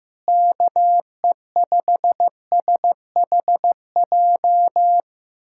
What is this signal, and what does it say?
Morse code, heard as a keyed tone, about 15 wpm, 700 Hz. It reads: KE5SHJ
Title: USA Callsign DE Callsign 15wpm